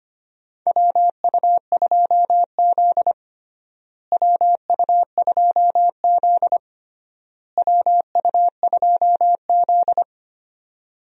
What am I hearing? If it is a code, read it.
Morse code, heard as a keyed tone, about 25 wpm, 700 Hz. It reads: WU2Z WU2Z WU2Z
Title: USA Call Signs - 4 Characters - Repeated 3x - 25wpm